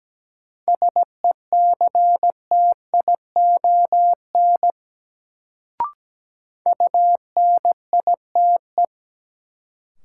Morse code, heard as a keyed tone, about 17 wpm, 700 Hz. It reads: SECTION UNITE
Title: Sentences from Top 1000 Words - Review Part 5 - New Word Drill - 17wpm